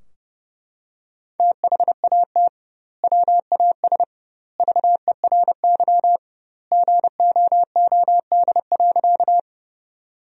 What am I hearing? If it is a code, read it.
Morse code, heard as a keyed tone, about 32 wpm, 700 Hz. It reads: THAT WAS VERY GOOD.
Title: 2x Word-Spacing - Sentences from Top 200 Words - 30wpm